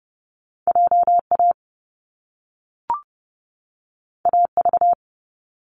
Morse code, heard as a keyed tone, about 30 wpm, 700 Hz. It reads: JA AV
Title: Top Two Letter Combinations in Words 30wpm